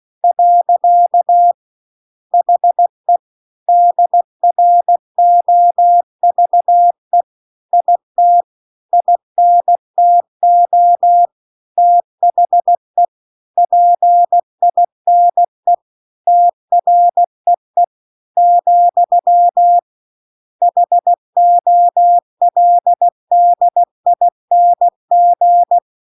Morse code, heard as a keyed tone, about 16 wpm, 700 Hz. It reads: . HE DROVE IT INTO THE PINE TREE , HOLDING